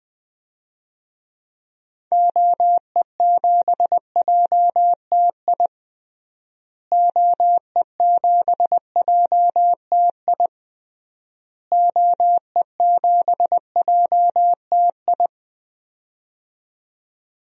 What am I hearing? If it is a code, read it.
Morse code, heard as a keyed tone, about 20 wpm, 700 Hz. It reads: OE7JTI OE7JTI OE7JTI
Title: DXCC-100 Call Signs - 6 Characters - Repeated 3x - 20wpm